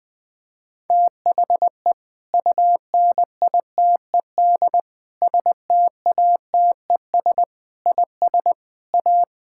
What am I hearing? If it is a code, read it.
Morse code, heard as a keyed tone, about 20 wpm, 700 Hz. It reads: THE UNITED STATES IS A
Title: Sentences with 4-Word Phrases - 20wpm